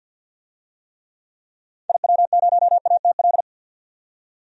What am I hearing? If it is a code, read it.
Morse code, heard as a keyed tone, about 47 wpm, 700 Hz. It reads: IW0ATL
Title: DXCC-100 Call Signs - 6 Characters - 50wpm